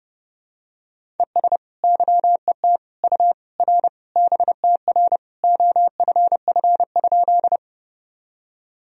Morse code, heard as a keyed tone, about 32 wpm, 700 Hz. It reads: ES YET U R BTR OFF?
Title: Sentences from Top 500 Words and Abbreviations 30wpm